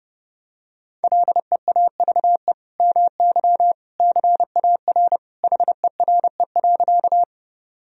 Morse code, heard as a keyed tone, about 30 wpm, 700 Hz. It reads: LEAVE MY CAR HERE.
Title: Sentences from Top 1000 Words - Review Part 1 - 30wpm